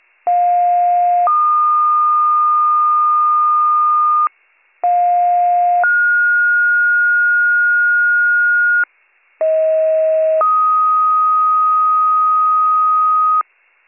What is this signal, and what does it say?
Two-tone page, heard as a one-second tone, then a three-second tone, about 710 then 1220 Hz, about 710 then 1430 Hz, about 640 then 1250 Hz. It is heard from 0.3 to 13.4 s.